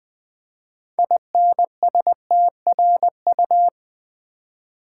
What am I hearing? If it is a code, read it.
Morse code, heard as a keyed tone, about 20 wpm, 700 Hz. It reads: INSTRU
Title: Build A Word - 6 Letters At A Time - 12-Letter Words - 20wpm